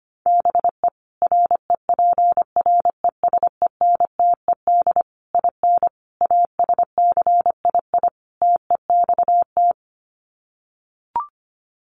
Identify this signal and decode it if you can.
Morse code, heard as a keyed tone, about 25 wpm, 700 Hz. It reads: BE REPRESENTED IN ASCII TEXT
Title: Sets of 5 Words 25wpm